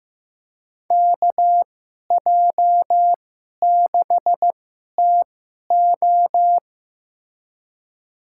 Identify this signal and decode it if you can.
Morse code, heard as a keyed tone, about 15 wpm, spaced as for 11 wpm, 700 Hz. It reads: KJ6TO
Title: USA Call Signs - 2x Character Spacing - 15wpm